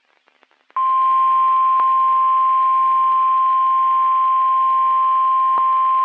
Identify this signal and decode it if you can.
Emergency Alert System (SAME) tones (the 1050 Hz attention tone). It is heard from 0.7 to 5.6 s.
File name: eas messages / EAS WXM31 Elmira